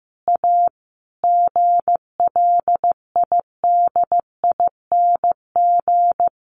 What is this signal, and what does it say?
Morse code, heard as a keyed tone, about 15 wpm, 700 Hz. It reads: A GLIDING
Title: A Sound Of Thunder 15wpm